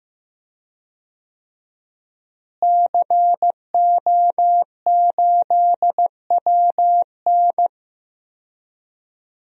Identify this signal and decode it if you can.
Morse code, heard as a keyed tone, about 15 wpm, 700 Hz. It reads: CO8WN
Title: DXCC-100 Call Signs - 5 Characters - Repeated 2x - 15wpm